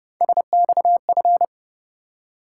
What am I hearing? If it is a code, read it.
Morse code, heard as a keyed tone, about 29 wpm, 700 Hz. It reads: SXF